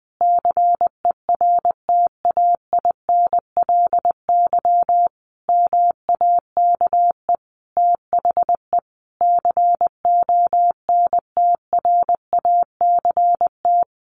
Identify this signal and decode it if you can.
Morse code, heard as a keyed tone, about 20 wpm, 700 Hz. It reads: CERTAINLY MAKE THE CONTRACT
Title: Sentences from Top 1400 Words 20wpm